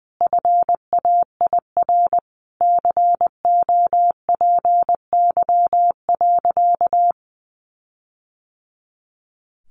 Morse code, heard as a keyed tone, about 20 wpm, 700 Hz. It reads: FAIR COPY.
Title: Sentences from Top 1400 Words 20wpm